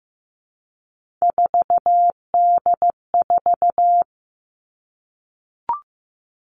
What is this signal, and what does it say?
Morse code, heard as a keyed tone, about 15 wpm, 700 Hz. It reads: 4D4
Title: All Cut Numbers - 3-Digit 15wpm